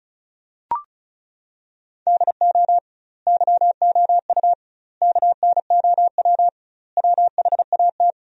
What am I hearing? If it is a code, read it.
Morse code, heard as a keyed tone, about 36 wpm, 700 Hz. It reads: DO YOU KNOW WHAT
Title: 2x Word-Spacing - Sentences from Top 100 Words - 35wpm